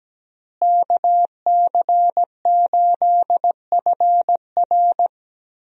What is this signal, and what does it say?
Morse code, heard as a keyed tone, about 17 wpm, 700 Hz. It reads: KC8FR
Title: USA Call Signs - 5 Characters - Repeated 2x - 17wpm